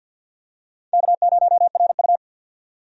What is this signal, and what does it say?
Morse code, heard as a keyed tone, about 50 wpm, 700 Hz. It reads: K0RU
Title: USA Call Signs - 4 Characters - 50wpm